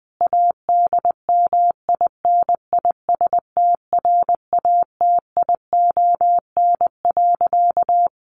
Morse code, heard as a keyed tone, about 20 wpm, 700 Hz. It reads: ADMINISTRATION.
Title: Sentences from Top 1300 Words 20wpm